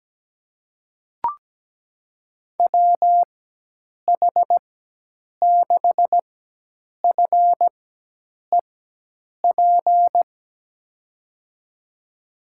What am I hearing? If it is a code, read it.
Morse code, heard as a keyed tone, about 17 wpm, spaced as for 8 wpm, 700 Hz. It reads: WH6FEP
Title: USA Call Signs - 6 Characters - 4x Character Spacing - 17wpm